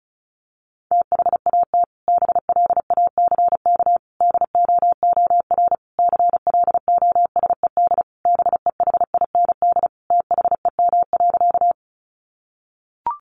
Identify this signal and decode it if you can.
Morse code, heard as a keyed tone, about 34 wpm, 700 Hz. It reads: THAT BLACK DOOR CLOSED BEHIND THEM.
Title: Sentences from Top 1400 Words 35wpm